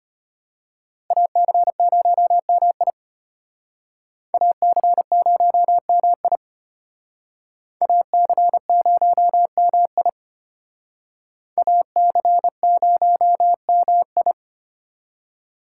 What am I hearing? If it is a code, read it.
Morse code, heard as a keyed tone, about 32 wpm, 700 Hz. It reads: AC0MI AC0MI AC0MI AC0MI